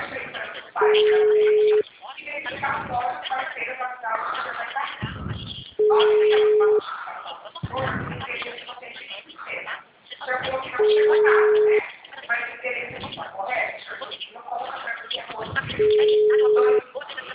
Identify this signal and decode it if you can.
Ringback tone (Europe, 425 Hz), with speech.
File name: out- -2007- - - .14249